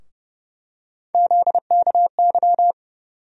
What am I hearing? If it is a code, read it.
Morse code, heard as a keyed tone, about 30 wpm, 700 Hz. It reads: ZKY
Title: USA Call Sign Postfixes - 3 Letters 30wpm